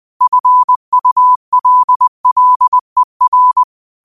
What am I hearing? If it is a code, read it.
Morse code, heard as a keyed tone, about 20 wpm, 1000 Hz. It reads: FULLER
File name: qotdfiles / 20wpm